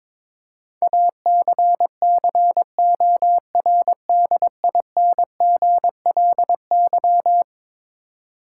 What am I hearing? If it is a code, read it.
Morse code, heard as a keyed tone, about 22 wpm, 700 Hz. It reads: ACCORDINGLY
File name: practice / 22wpm / Eleven Letter Words - 22wpm